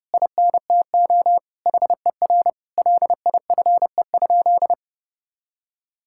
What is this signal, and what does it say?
Morse code, heard as a keyed tone, about 30 wpm, 700 Hz. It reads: INTO HER LIFE?
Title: Sentences from Top 1000 Words - Review Part 6 - 30wpm